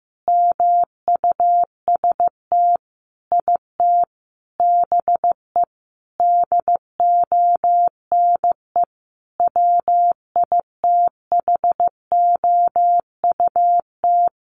Morse code, heard as a keyed tone, about 15 wpm, 700 Hz. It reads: MUST IT BE DONE WITHOUT
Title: Sentences from Top 600 Words 15wpm